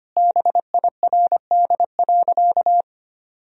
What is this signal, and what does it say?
Morse code, heard as a keyed tone, about 25 wpm, 700 Hz. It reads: BIRD.
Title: Sentences from Top 1000 Words - Review Part 2 - 25wpm